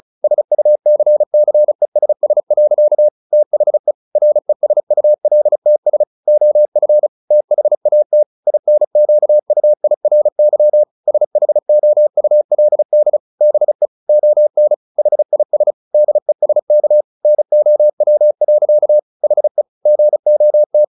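Morse code, heard as a keyed tone, about 34 wpm, 600 Hz. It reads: SUCCESS. THE RESULTS OF THAT INQUIRY SHOULD BE ON HIS DESK NOW. HE GOT
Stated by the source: Morse practice at 35 wpm from CW med Gnister, sending Ken Follett's novel Never in English.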